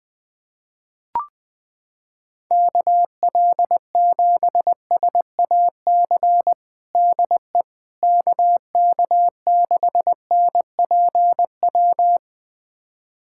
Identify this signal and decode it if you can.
Morse code, heard as a keyed tone, about 20 wpm, 700 Hz. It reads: KL7SAC DE KK6NPW
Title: USA Callsign DE Callsign - 6 Characters - 20wpm